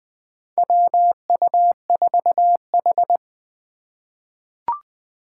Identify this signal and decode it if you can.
Morse code, heard as a keyed tone, about 20 wpm, 700 Hz. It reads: WU4H
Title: USA Call Signs - 4 Characters - 4x Character Spacing - 20wpm